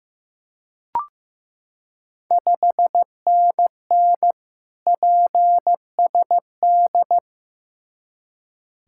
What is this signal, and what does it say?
Morse code, heard as a keyed tone, about 15 wpm, 690 Hz. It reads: 5NN PSD
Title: Ontario QSO Party Exchange 15wpm